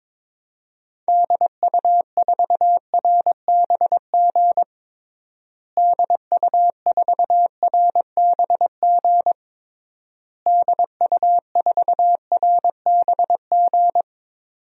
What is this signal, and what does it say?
Morse code, heard as a keyed tone, about 22 wpm, 700 Hz. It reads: DU4RBG DU4RBG DU4RBG
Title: DXCC-100 Call Signs - 6 Characters - Repeated 3x - 22wpm